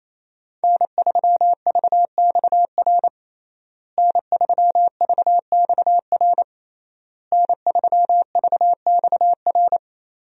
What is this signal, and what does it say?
Morse code, heard as a keyed tone, about 28 wpm, 700 Hz. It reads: N3VXR N3VXR N3VXR
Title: USA Call Signs - 5 Characters - Repeated 3x - 28wpm